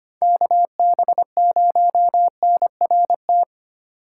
Morse code, heard as a keyed tone, about 25 wpm, 700 Hz. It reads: KB0NRT
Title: USA Call Signs - 6 Characters - 4x Character Spacing - 25wpm